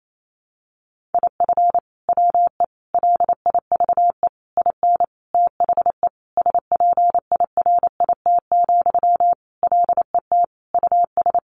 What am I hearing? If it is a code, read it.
Morse code, heard as a keyed tone, about 28 wpm, 700 Hz. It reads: IF WE LIVE IN THE SPIRIT, LET US
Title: Sentences from Top 1400 Words 28wpm